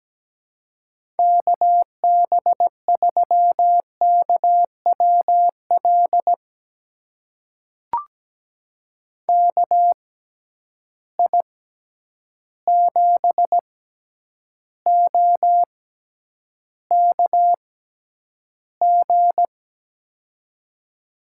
Morse code, heard as a keyed tone, about 17 wpm, spaced as for 9 wpm, 700 Hz. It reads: KB3KWL KI7OKG
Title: USA Call Signs - 6 Characters - 6x Character Spacing - 17wpm